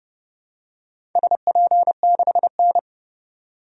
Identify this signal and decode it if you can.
Morse code, heard as a keyed tone, about 30 wpm, 700 Hz. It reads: SP6N
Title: DXCC-100 Call Signs - 4 Characters - 30wpm